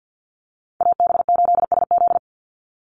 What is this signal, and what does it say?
Morse code, heard as a keyed tone, about 50 wpm, 700 Hz. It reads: AB8SZ
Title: USA Call Signs 50wpm